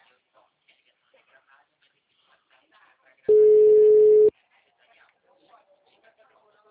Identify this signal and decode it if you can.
Ringback tone (Europe, 425 Hz), with speech.